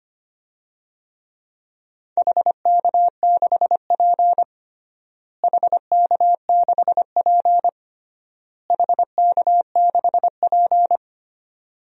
Morse code, heard as a keyed tone, about 25 wpm, 700 Hz. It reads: HK6P HK6P HK6P
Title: DXCC-100 Call Signs - 4 Characters - Repeated 3x - 25wpm